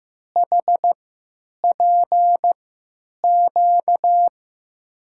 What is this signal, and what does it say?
Morse code, heard as a keyed tone, about 15 wpm, spaced as for 9 wpm, 700 Hz. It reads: HPQ
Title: USA Call Signs - 6 Characters - 3x Character Spacing - 15wpm